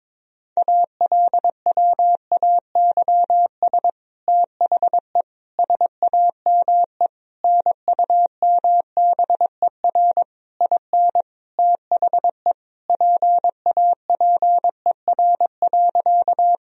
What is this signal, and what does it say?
Morse code, heard as a keyed tone, about 21 wpm, 700 Hz. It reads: ALWAYS THE SAME NUMBER IN THE PAPER.
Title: Sentences from Top 1000 Words - Review Part 1 - 22wpm